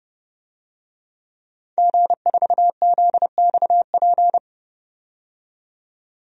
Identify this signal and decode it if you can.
Morse code, heard as a keyed tone, about 30 wpm, 700 Hz. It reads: G4ZXP
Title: DXCC-100 Call Signs - 5 Characters - 30wpm